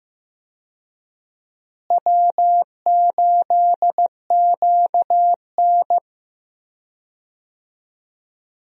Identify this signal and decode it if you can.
Morse code, heard as a keyed tone, about 15 wpm, 700 Hz. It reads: W8QN
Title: USA Call Signs - 4 Characters - Repeated 3x - 15wpm